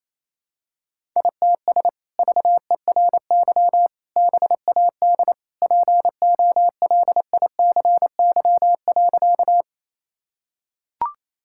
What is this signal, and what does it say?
Morse code, heard as a keyed tone, about 28 wpm, 700 Hz. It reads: ITS VERY BAD POLICY.
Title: Sentences from Top 1000 Words - Review Part 5 - 28wpm